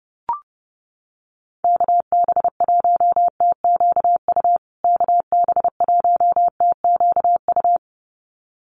Morse code, heard as a keyed tone, about 28 wpm, 700 Hz. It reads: KB1TQU KB1TQU
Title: USA Call Signs Repeated 2x 30wpm